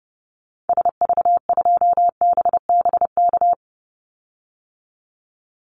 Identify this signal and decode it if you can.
Morse code, heard as a keyed tone, about 30 wpm, 700 Hz. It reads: SV2BBK
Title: Call Signs Repeated 2x - 30wpm